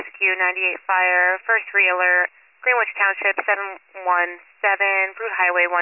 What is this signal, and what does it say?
Emergency Alert System (SAME) tones, heard as harsh buzzing bursts, with speech. It is heard from 1.2 to 5.1 s.